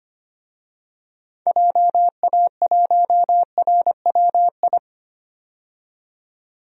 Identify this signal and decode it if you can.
Morse code, heard as a keyed tone, about 25 wpm, 700 Hz. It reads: JA1RWI
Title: DXCC-100 Call Signs - 6 Characters - 25wpm